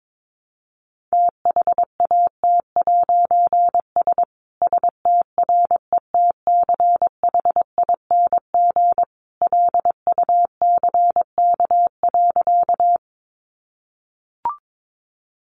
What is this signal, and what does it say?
Morse code, heard as a keyed tone, about 22 wpm, 700 Hz. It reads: THAT'S STRETCHING LUCK.
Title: A Sound Of Thunder 22wpm